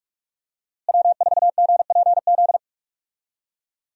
Morse code, heard as a keyed tone, about 46 wpm, 700 Hz. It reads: W4GPZ
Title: USA Call Signs - 5 Characters - Repeated 3x - 45wpm